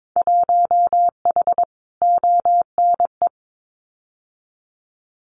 Morse code, heard as a keyed tone, about 22 wpm, 700 Hz. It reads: 1H ONE
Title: Winter Field Day Exchanges Encore 22wpm